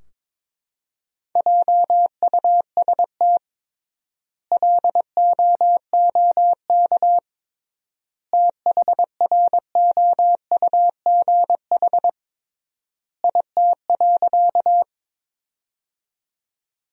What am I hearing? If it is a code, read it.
Morse code, heard as a keyed tone, about 22 wpm, 700 Hz. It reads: JUST LOOK THROUGH IT.
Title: 3x Word-Spacing - Sentences from Top 200 Words - 22wpm